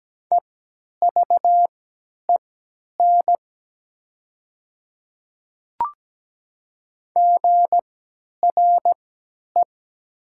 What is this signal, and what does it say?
Morse code, heard as a keyed tone, about 17 wpm, spaced as for 10 wpm, 700 Hz. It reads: EVEN GRE
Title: 3x Character Spacing - Top 500 Words - 17wpm